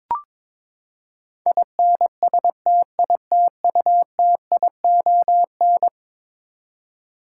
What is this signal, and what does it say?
Morse code, heard as a keyed tone, about 22 wpm, 700 Hz. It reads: INSTITUTION
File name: practice / 22wpm / Eleven Letter Words - 22wpm